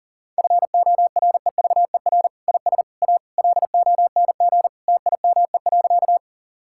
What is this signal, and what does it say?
Morse code, heard as a keyed tone, about 40 wpm, 700 Hz. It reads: FOREVER IS A LONG TIME.
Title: Sentences from Top 1600 Words 40wpm